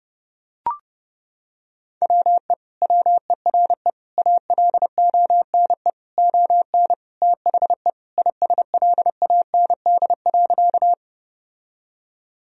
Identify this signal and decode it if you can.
Morse code, heard as a keyed tone, about 30 wpm, 700 Hz. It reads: WE WERE ALONE ON THE ISLAND.
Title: Sentences from Top 1000 Words - Review Part 3 - 30wpm